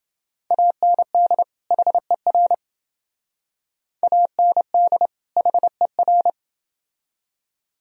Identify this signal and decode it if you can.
Morse code, heard as a keyed tone, about 28 wpm, 700 Hz. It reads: AND HER AND HER